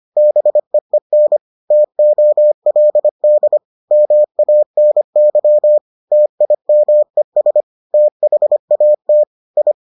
Morse code, heard as a keyed tone, about 25 wpm, 600 Hz. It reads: BEEN TOLD MANY TIMES THAT I